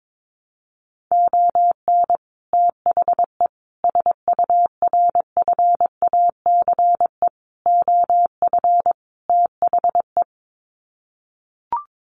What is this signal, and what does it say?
Morse code, heard as a keyed tone, about 22 wpm, 700 Hz. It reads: ON THE SURFACE OF THE
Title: Sets of 5 Words Encore 22wpm